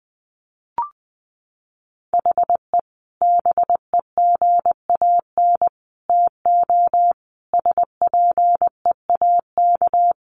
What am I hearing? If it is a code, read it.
Morse code, heard as a keyed tone, about 20 wpm, 700 Hz. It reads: HE BEGAN TO SPEAK
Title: Sentences from Top 1100 Words 20wpm